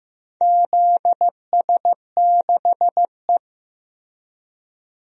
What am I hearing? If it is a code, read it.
Morse code, heard as a keyed tone, about 15 wpm, 700 Hz. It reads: ZS6E
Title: DXCC-100 Call Signs - 4 Characters - Repeated 2x - 15wpm